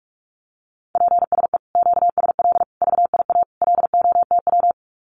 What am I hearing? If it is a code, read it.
Morse code, heard as a keyed tone, about 46 wpm, 700 Hz. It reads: PSE QSL VIA LOTW
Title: Common QSO Phrases Encore 45wpm